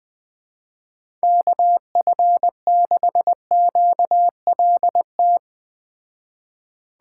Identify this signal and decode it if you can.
Morse code, heard as a keyed tone, about 20 wpm, 700 Hz. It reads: KF6QLT
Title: Speed Racing - USA Call Signs 17wpm